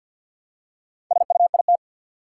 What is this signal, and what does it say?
Morse code, heard as a keyed tone, about 52 wpm, 700 Hz. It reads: SUIT
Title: Four Letter Words - 50wpm